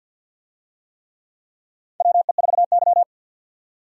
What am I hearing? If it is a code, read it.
Morse code, heard as a keyed tone, about 50 wpm, 700 Hz. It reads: WE4Y
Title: USA Call Signs - 4 Characters - 50wpm